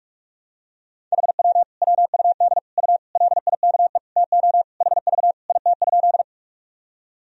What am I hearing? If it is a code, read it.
Morse code, heard as a keyed tone, about 43 wpm, 700 Hz. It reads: HW WUD U LIKE TO HV IT?
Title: Sentences from Top 100 Words and Abbreviations 45wpm